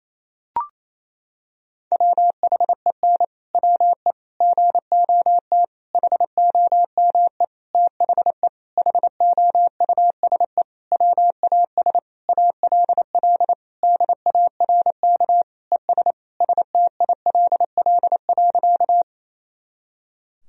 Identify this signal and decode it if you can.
Morse code, heard as a keyed tone, about 28 wpm, 700 Hz. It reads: WHEN WE GOT HOME THE HOUSE WAS ALL DARK ES STILL.
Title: Sentences from Top 500 Words and Abbreviations 28wpm